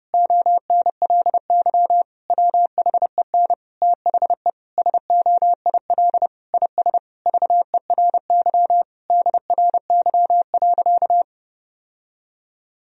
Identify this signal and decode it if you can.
Morse code, heard as a keyed tone, about 32 wpm, 700 Hz. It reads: ONLY WHEN THE SOIL IS VERY DRY.
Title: Sentences from Top 1000 Words - Review Part 4 - 30wpm